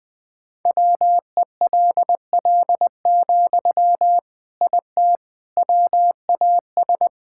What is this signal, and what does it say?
Morse code, heard as a keyed tone, about 20 wpm, 700 Hz. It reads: WELL, IT WAS
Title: Sentences from Top 1000 Words - Review Part 6 - 20wpm